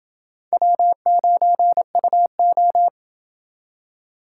Morse code, heard as a keyed tone, about 28 wpm, 700 Hz. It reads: W9UO